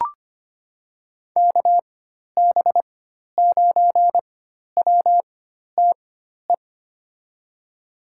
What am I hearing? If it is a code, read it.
Morse code, heard as a keyed tone, about 25 wpm, spaced as for 12 wpm, 700 Hz. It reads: KB9WTE
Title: USA Call Signs - 6 Characters - 4x Character Spacing - 25wpm